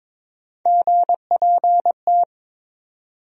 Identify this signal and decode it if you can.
Morse code, heard as a keyed tone, about 22 wpm, 700 Hz. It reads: GPT
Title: USA Call Sign Postfixes - 3 Letters 22wpm